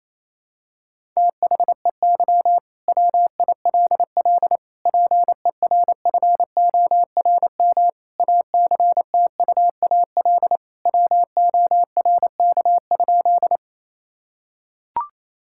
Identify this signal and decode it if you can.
Morse code, heard as a keyed tone, about 28 wpm, 700 Hz. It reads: THEY WILL PERFORM ACTUAL WORK?
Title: Sentences from Top 1000 Words - Review Part 5 - 28wpm